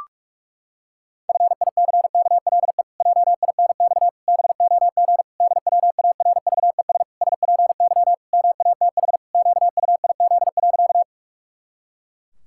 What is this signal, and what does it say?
Morse code, heard as a keyed tone, about 46 wpm, 700 Hz. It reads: FICKLE JINX BOG DWARVES SPY MATH QUIZ.
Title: Pangram Challenge v2 - 45wpm